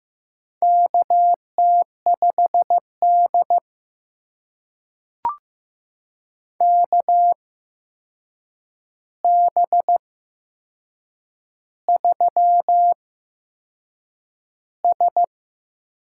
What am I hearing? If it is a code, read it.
Morse code, heard as a keyed tone, about 15 wpm, spaced as for 6 wpm, 700 Hz. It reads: KT5D KB3S
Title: USA Call Signs - 4 Characters - 8x Character Spacing - 15wpm